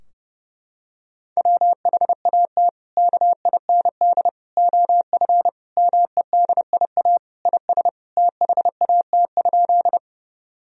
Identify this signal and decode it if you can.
Morse code, heard as a keyed tone, about 30 wpm, 700 Hz. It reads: WHAT KIND OF MEDIA IS THAT?
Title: Sentences from Top 1000 Words - Review Part 4 - 30wpm